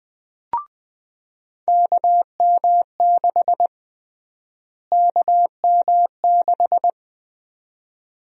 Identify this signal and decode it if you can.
Morse code, heard as a keyed tone, about 21 wpm, 700 Hz. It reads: KM6 KM6
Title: USA Call Sign Prefixes - Repeated 2x - 20wpm